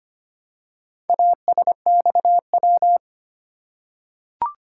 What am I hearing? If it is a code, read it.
Morse code, heard as a keyed tone, about 25 wpm, 700 Hz. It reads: ASXW
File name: practice / 25wpm / Four Letter-Number Combinations - 25wpm